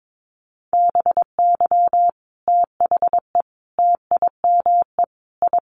Morse code, heard as a keyed tone, about 22 wpm, 700 Hz. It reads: BY THE TIME I
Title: Sets of 5 Words Encore2 22wpm